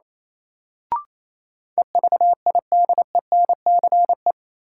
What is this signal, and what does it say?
Morse code, heard as a keyed tone, about 28 wpm, 700 Hz. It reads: EVIDENCE
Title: Sentences from Top 1300 Words - New Word Drill Alt - 28wpm